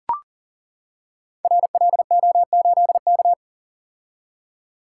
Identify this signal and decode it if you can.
Morse code, heard as a keyed tone, about 42 wpm, 690 Hz. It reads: RLO8K
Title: Code Groups - 5 Letter:Numbers - 40wpm